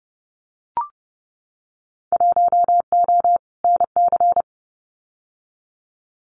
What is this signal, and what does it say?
Morse code, heard as a keyed tone, about 30 wpm, 700 Hz. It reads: 1O NC
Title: Winter Field Day Exchanges 30wpm